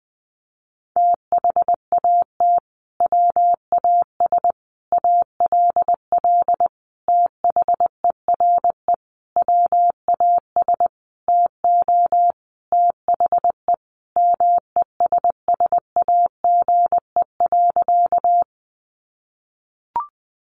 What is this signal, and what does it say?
Morse code, heard as a keyed tone, about 19 wpm, 700 Hz. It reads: THAT WAS ALL THERE WAS TO THE MESSAGE.
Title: Sentences from Top 1100 Words 20wpm